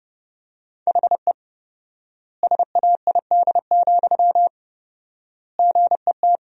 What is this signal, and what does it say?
Morse code, heard as a keyed tone, about 30 wpm, 700 Hz. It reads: HE SAID, GET
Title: 4x Word-Spacing - Sentences from Top 400 Words - 30wpm